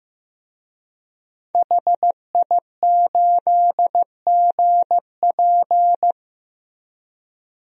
Morse code, heard as a keyed tone, about 15 wpm, 700 Hz. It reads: HI8GP
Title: DXCC-100 Call Signs - 5 Characters - Repeated 2x - 15wpm